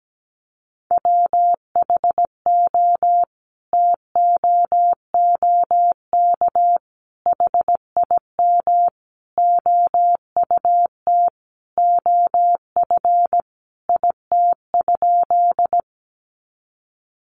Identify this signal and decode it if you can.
Morse code, heard as a keyed tone, about 17 wpm, 700 Hz. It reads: WHO TOOK HIM OUT OF IT?
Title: Sentences from Top 300 Words 17wpm